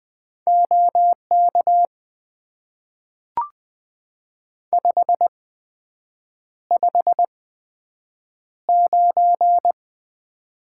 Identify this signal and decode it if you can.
Morse code, heard as a keyed tone, about 20 wpm, spaced as for 7 wpm, 700 Hz. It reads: OK 559
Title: Top 90 QSO Elements - 8x Character Spacing - 20wpm